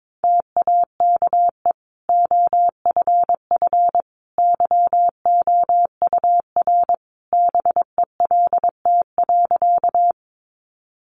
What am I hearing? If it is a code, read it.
Morse code, heard as a keyed tone, about 22 wpm, 700 Hz. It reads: TAKE OFF YOUR BELT.
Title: Sentences from Top 1400 Words 22wpm